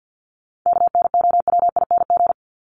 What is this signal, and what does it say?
Morse code, heard as a keyed tone, about 48 wpm, 700 Hz. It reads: KNOWING
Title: ING Suffix - Top 100 Words with Base Words 50wpm